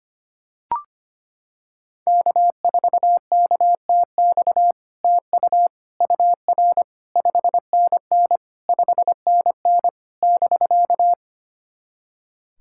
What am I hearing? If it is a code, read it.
Morse code, heard as a keyed tone, about 25 wpm, 700 Hz. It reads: K4KTX TU UR 5NN 5NN <BK>
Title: SOTA QSO Practice 25wpm